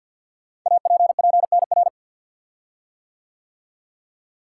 Morse code, heard as a keyed tone, about 46 wpm, 700 Hz. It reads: APPNR
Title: Code Groups - 5 Letter:Numbers - 50wpm